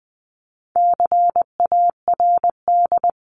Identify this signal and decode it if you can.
Morse code, heard as a keyed tone, about 20 wpm, 700 Hz. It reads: CARD
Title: Top 501-750 Words 20wpm